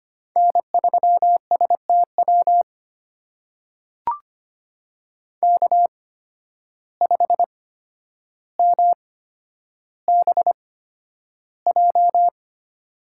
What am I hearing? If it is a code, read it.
Morse code, heard as a keyed tone, about 25 wpm, spaced as for 11 wpm, 700 Hz. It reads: N3STW K5MBJ
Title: USA Call Signs - 5 Characters - 8x Character Spacing - 25wpm